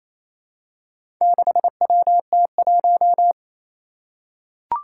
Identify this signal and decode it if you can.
Morse code, heard as a keyed tone, about 28 wpm, 700 Hz. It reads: 6WT1
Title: Four Challenging Letter-Number Combinations 28wpm